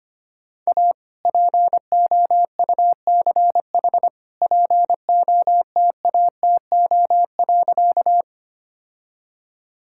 Morse code, heard as a keyed tone, about 25 wpm, 700 Hz. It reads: A POUCH POTATO.
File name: practice / 25wpm / All-Age Appropriate Jokes - 25wpm